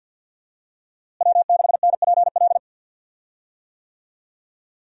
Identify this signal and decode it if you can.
Morse code, heard as a keyed tone, about 51 wpm, 700 Hz. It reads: W6NPL
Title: USA Call Signs - 5 Characters - Repeated 3x - 50wpm